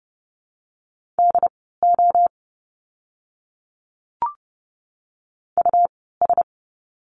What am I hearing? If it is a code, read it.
Morse code, heard as a keyed tone, about 30 wpm, spaced as for 17 wpm, 700 Hz. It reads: DO US
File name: course / 30:10wpm / 115 Lesson 19f - Letter w - Words 30wpm